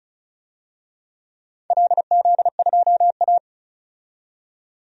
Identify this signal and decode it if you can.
Morse code, heard as a keyed tone, about 36 wpm, 700 Hz. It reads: LZ2A
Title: DXCC-100 Call Signs - 4 Characters - 35wpm